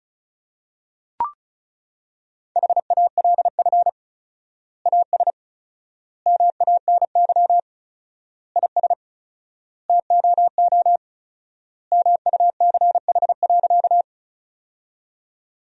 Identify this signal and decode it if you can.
Morse code, heard as a keyed tone, about 36 wpm, 700 Hz. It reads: HALF AS MANY IS TOO MUCH.
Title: 4x Word-Spacing - Sentences from Top 400 Words - 35wpm